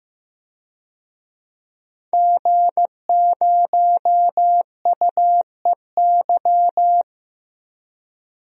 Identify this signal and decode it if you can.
Morse code, heard as a keyed tone, about 15 wpm, 700 Hz. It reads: G0UEY
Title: DXCC-100 Call Signs - 5 Characters - Repeated 2x - 15wpm